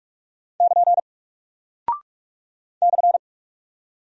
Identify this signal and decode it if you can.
Morse code, heard as a keyed tone, about 43 wpm, 700 Hz. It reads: <KN> /
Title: Single Character (Letter:Number:Punctuation:Prosign) Alt 45wpm